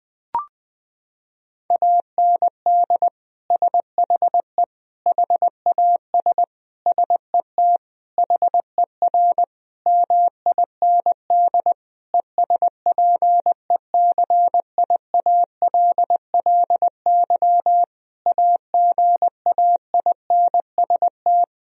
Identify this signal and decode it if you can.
Morse code, heard as a keyed tone, about 20 wpm, 700 Hz. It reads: AND SHE HAS SET HER MIND ESPECIALLY AGAINST
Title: Sentences from Top 1000 Words - Review Part 4 - 20wpm